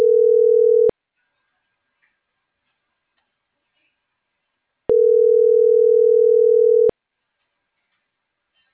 Ringback tone (North America), 440 + 480 Hz.